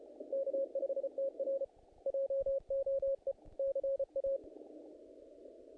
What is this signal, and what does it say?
Morse code, heard as a keyed tone, about 33 wpm, 530 Hz. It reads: K5TR JOE CA
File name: contests / naqp-cw-k5tr-op-k5pi- -2000